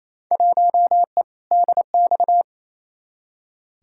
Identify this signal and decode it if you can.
Morse code, heard as a keyed tone, about 28 wpm, 700 Hz. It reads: 1E DX
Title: Field Day Exchanges Encore - 28wpm